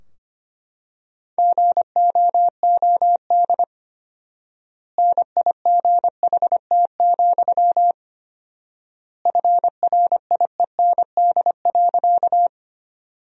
Morse code, heard as a keyed tone, about 25 wpm, 700 Hz. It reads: GOOD NIGHT, FRIEND.
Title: 4x Word-Spacing - Sentences from Top 300 Words - 25wpm